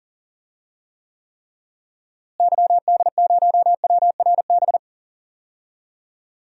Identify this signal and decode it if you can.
Morse code, heard as a keyed tone, about 37 wpm, 700 Hz. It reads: YD0WRB
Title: DXCC-100 Call Signs - 6 Characters - Repeated 3x - 40wpm